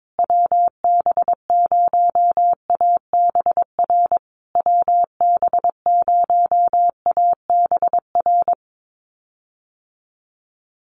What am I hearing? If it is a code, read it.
Morse code, heard as a keyed tone, about 22 wpm, 700 Hz. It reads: WB0ABR WB0ABR
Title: USA Call Signs Repeated 3x 22wpm